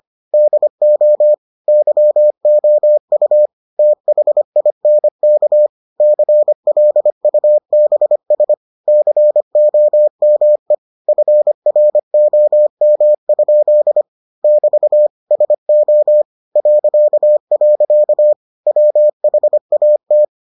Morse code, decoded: DO YOU THINK CLUBS COME FROM? <BT> SO .. WHAT

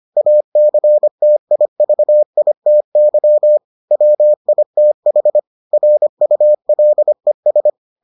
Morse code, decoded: ACTIVITY WITH RULES

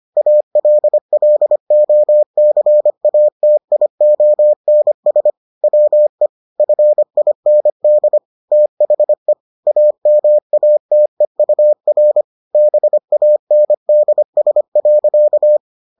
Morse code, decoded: ALLOCATIONS WE FIND THE AMATEUR BANDS.

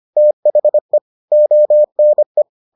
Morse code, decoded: THE ONE

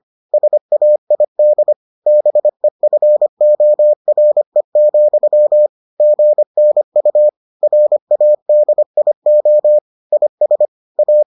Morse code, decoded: SAID BEFORE, GNU RADIO IS A

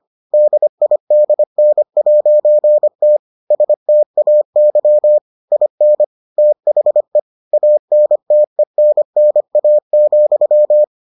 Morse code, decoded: DIDN'T STAY IN THE ANTENNA,

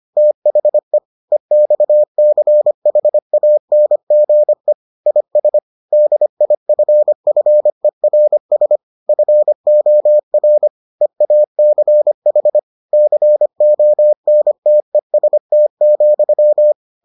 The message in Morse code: THE EXCHANGE IS DIFFERS FOR EACH CONTEST,

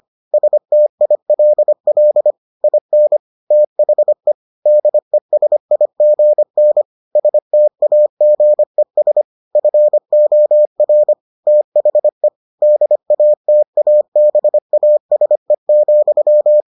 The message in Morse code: STILL IN THE DESIGN STAGES FOR THE DATABASE,

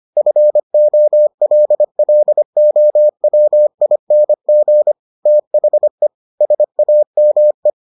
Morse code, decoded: FOLLOWING THE SAME

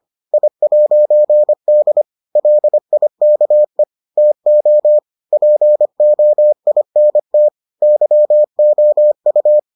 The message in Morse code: I'D LIKE TO POINT YOU

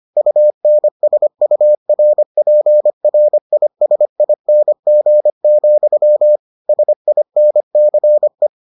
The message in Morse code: UNSURPRISING, SINCE

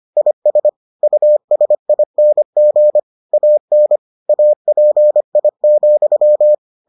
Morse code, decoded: IS USING AN API,

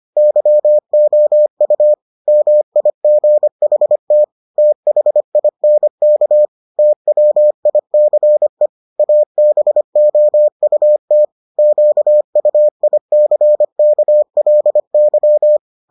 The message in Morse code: YOU MIGHT THINK TWICE ABOUT QUICKLY

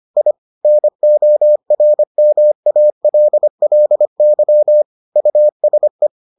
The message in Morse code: I NORMALLY USE

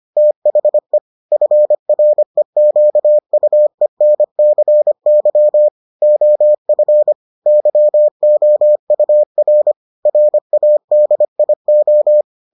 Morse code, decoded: THE FREQUENCY OF YOUR RADIO

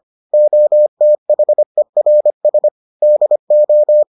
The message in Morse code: OTHERS DO

